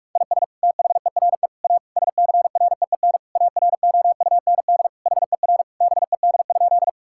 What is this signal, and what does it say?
Morse code, heard as a keyed tone, about 47 wpm, 700 Hz. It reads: IS THERE A SCREEN AROUND HER BED?
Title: Sentences from Top 1000 Words - Review Part 6 - 45wpm